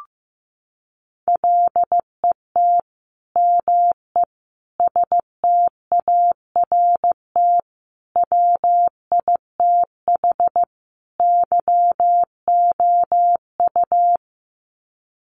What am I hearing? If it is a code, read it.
Morse code, heard as a keyed tone, about 15 wpm, 700 Hz. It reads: LET ME START WITH YOU
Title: Sets of 5 Words Encore 15wpm